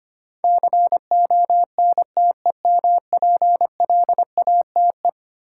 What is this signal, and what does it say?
Morse code, heard as a keyed tone, about 25 wpm, 700 Hz. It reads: CONTEMPLATE
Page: morse-code-ninja-practice-25wpm